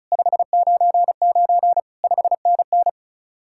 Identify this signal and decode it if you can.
Morse code, heard as a keyed tone, about 43 wpm, 700 Hz. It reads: 599 5NN
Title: Speed-Racing Common QSO Phrases - 35wpm